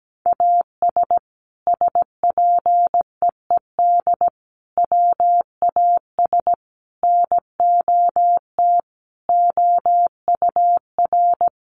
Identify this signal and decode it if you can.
Morse code, heard as a keyed tone, about 17 wpm, 700 Hz. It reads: AS SPEED WAS NOT OUR
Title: Sentences from Top 800 Words 17wpm